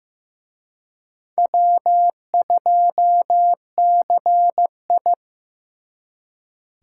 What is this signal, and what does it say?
Morse code, heard as a keyed tone, about 18 wpm, 700 Hz. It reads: W2CI
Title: Speed Racing - USA 1x2 Call Signs 15wpm